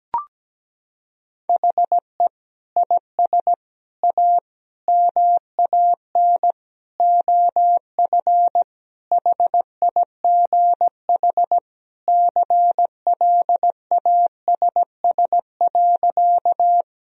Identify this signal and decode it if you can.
Morse code, heard as a keyed tone, about 17 wpm, 700 Hz. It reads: HE IS A MAN OF HIGH CLASS.
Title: Sentences from Top 1000 Words - Review Part 2 - 17wpm